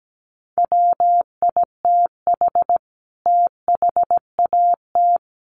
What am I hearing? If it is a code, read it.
Morse code, heard as a keyed tone, about 17 wpm, 700 Hz. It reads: WITH THAT
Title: Sentences from Top 1200 Words 17wpm